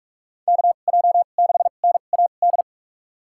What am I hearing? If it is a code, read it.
Morse code, heard as a keyed tone, about 46 wpm, spaced as for 32 wpm, 700 Hz. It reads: KJ6NAD
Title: USA Call Signs - 6 Characters - 2x Character Spacing - 45wpm